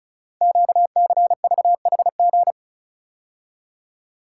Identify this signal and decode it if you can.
Morse code, heard as a keyed tone, about 36 wpm, 700 Hz. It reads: QCVHG
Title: Code Groups - 5 Letters - 35wpm